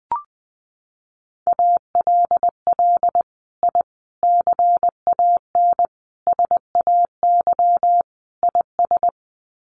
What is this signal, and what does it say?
Morse code, heard as a keyed tone, about 20 wpm, 700 Hz. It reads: ALL I CAN SAY IS
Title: Sets of 5 Words Encore2 20wpm